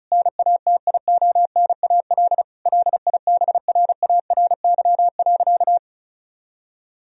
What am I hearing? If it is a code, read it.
Morse code, heard as a keyed tone, about 36 wpm, 700 Hz. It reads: NATIONAL LIBRARY.
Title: Sentences from Top 1000 Words - Review Part 4 - 35wpm